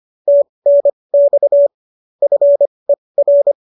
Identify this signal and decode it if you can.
Morse code, heard as a keyed tone, about 25 wpm, spaced as for 20 wpm, 570 Hz. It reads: TNX FER